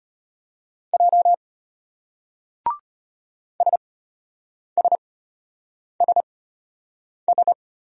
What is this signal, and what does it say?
Morse code, heard as a keyed tone, about 34 wpm, spaced as for 8 wpm, 700 Hz. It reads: J SSSS